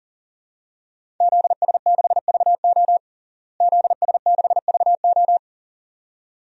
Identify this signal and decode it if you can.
Morse code, heard as a keyed tone, about 40 wpm, 700 Hz. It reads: ZS6VO ZS6VO
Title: DXCC-100 Call Signs - 5 Characters - Repeated 2x - 40wpm